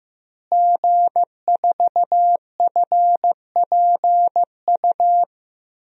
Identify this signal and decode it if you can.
Morse code, heard as a keyed tone, about 15 wpm, 700 Hz. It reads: G4FPU
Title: DXCC-100 Call Signs - 5 Characters - Repeated 2x - 15wpm